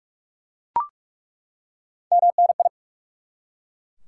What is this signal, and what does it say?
Morse code, heard as a keyed tone, about 43 wpm, 700 Hz. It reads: MNI
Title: Top 180 QSO Elements 45wpm